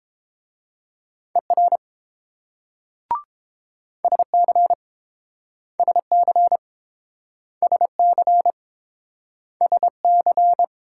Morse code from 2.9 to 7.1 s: SC SC

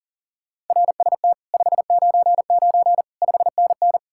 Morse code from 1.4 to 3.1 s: 599

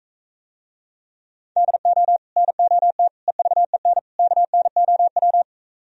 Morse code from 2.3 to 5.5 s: NOT EVEN KNOW

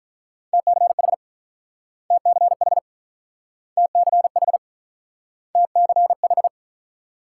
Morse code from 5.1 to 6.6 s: TCH

